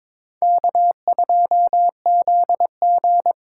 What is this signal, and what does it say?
Morse code, heard as a keyed tone, about 26 wpm, 700 Hz. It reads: K2ZG
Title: Speed Racing - USA 1x2 Call Signs 22wpm